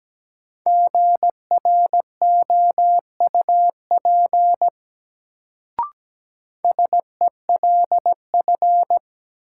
Morse code, heard as a keyed tone, about 17 wpm, 700 Hz. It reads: GROUP SELF
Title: Sentences from Top 1000 Words - Review Part 1 - New Word Drill - 17wpm